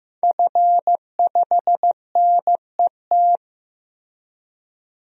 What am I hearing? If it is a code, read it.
Morse code, heard as a keyed tone, about 15 wpm, 700 Hz. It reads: F5NET
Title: DXCC-100 Call Signs - 5 Characters - Repeated 3x - 15wpm